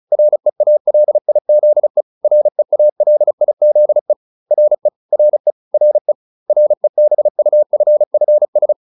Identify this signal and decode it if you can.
Morse code, heard as a keyed tone, about 34 wpm, 600 Hz. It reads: REALIZE REALIZE RE RE RE REBUFFS